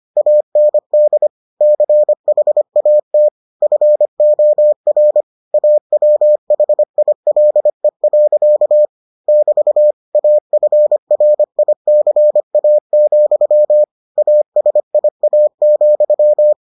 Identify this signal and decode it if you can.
Morse code, heard as a keyed tone, about 25 wpm, 600 Hz. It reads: AND CHAT FOR AWHILE. <BT> AFRICA, ASIA,